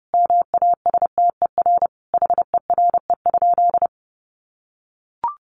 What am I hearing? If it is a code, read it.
Morse code, heard as a keyed tone, about 30 wpm, 700 Hz. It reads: MASTER HERE?
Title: Sentences from Top 1300 Words 30wpm